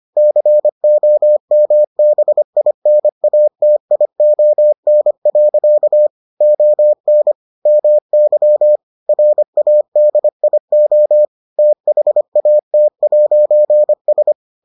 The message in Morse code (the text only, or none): COMBINATION. ON MY RADIO THAT'S